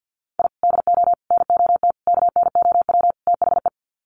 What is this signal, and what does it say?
Morse code, heard as a keyed tone, about 50 wpm, 700 Hz. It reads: I DO NOT KNOW THE
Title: Sets of 5 Words 50wpm